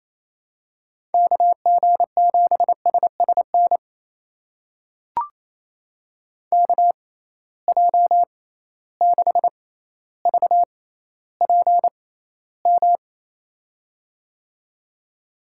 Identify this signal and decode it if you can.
Morse code, heard as a keyed tone, about 28 wpm, spaced as for 14 wpm, 700 Hz. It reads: KG7SSN KJ6VPM
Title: USA Call Signs - 6 Characters - 6x Character Spacing - 28wpm